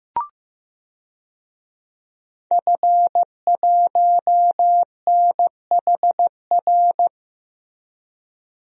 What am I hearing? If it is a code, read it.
Morse code, heard as a keyed tone, about 15 wpm, 700 Hz. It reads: F1NHR
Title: DXCC-100 Call Signs - 5 Characters - Repeated 2x - 15wpm